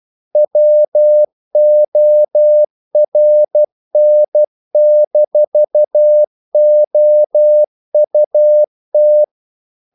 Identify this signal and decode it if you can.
Morse code, heard as a keyed tone, about 12 wpm, 600 Hz. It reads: WORN-OUT
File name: CW / Practice / If12